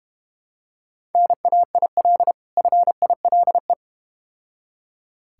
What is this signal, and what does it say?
Morse code, heard as a keyed tone, about 30 wpm, 700 Hz. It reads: NAIL FILE
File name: practice / 25wpm / Difficult Words - Speed-Racing with Hints - 25wpm